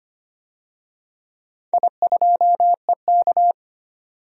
Morse code, heard as a keyed tone, about 25 wpm, 700 Hz. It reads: I2EK
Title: DXCC-100 Call Signs - 4 Characters - 25wpm